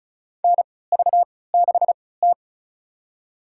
Morse code, heard as a keyed tone, about 36 wpm, spaced as for 20 wpm, 700 Hz. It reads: NV6T
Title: USA Call Signs - 4 Characters - 3x Character Spacing - 35wpm